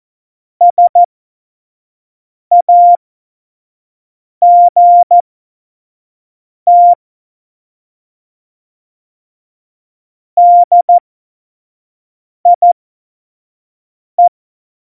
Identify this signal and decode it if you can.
Morse code, heard as a keyed tone, about 14 wpm, spaced as for 5 wpm, 700 Hz. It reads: SAGT DIE
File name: Morsekurs / Witze / Witz 10.Juli 2018, 25 F70